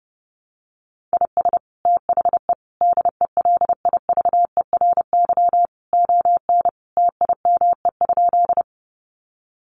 Morse code, heard as a keyed tone, about 30 wpm, 700 Hz. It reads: IS THE DELIVERY ON TIME?
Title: Sentences from Top 1500 Words 30wpm